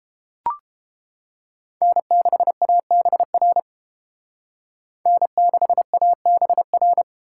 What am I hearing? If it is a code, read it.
Morse code, heard as a keyed tone, about 30 wpm, 700 Hz. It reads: N6ABR N6ABR